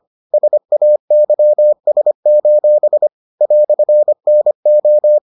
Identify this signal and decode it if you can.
Morse code, heard as a keyed tone, about 25 wpm, 600 Hz. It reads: SAYS: "NO